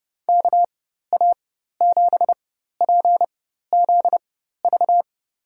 Morse code, heard as a keyed tone, about 31 wpm, spaced as for 14 wpm, 700 Hz. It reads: KA7PZV
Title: USA Call Signs - 6 Characters - 4x Character Spacing - 30wpm